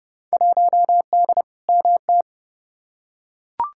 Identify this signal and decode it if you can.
Morse code, heard as a keyed tone, about 29 wpm, 700 Hz. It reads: 1D MT
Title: Field Day Exchanges - 30wpm